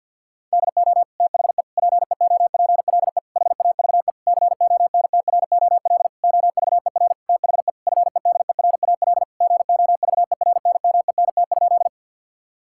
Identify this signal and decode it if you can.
Morse code, heard as a keyed tone, about 50 wpm, 700 Hz. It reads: DO THE PEOPLE HAVE CONTROL OVER THE FEDERAL GOVERNMENT?
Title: Sentences from Top 1000 Words - Review Part 5 - 50wpm